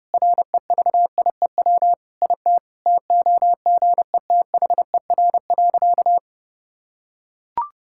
Morse code, heard as a keyed tone, about 30 wpm, 700 Hz. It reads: REVIEW IT TOGETHER.
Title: Sentences from Top 1000 Words - Review Part 6 - 30wpm